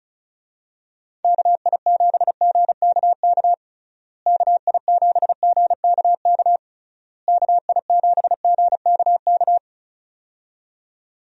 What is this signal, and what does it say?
Morse code, heard as a keyed tone, about 34 wpm, 700 Hz. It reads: KI7GKK KI7GKK KI7GKK
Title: USA Call Signs - 6 Characters - Repeated 3x - 35wpm